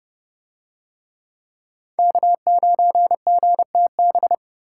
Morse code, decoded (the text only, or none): K9GTB